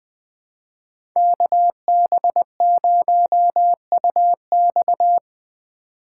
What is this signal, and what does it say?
Morse code, heard as a keyed tone, about 20 wpm, 700 Hz. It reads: KB0UX
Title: USA Call Signs - 3x Character Spacing - 20wpm